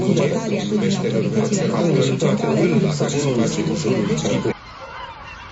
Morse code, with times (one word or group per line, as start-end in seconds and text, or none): none